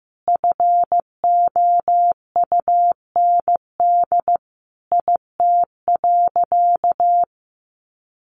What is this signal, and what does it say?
Morse code, heard as a keyed tone, about 15 wpm, 700 Hz. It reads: FOUND IT.
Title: Sentences from Top 300 Words 15wpm